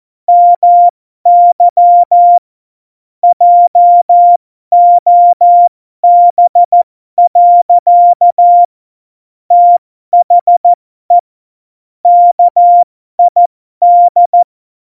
Morse code, heard as a keyed tone, about 14 wpm, spaced as for 12 wpm, 700 Hz. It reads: MY JOB. THE KID